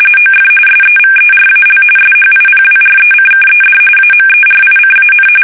RTTY, about 45.45 baud. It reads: OF SEVERAL SHIPWRECKED AMERICAN
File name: fsk850 / usnavy3